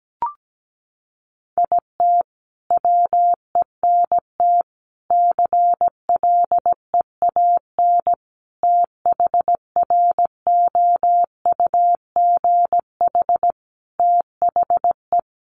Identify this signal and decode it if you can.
Morse code, heard as a keyed tone, about 17 wpm, 700 Hz. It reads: IT WENT CLEAN THROUGH THE
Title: Sentences from Top 1200 Words 17wpm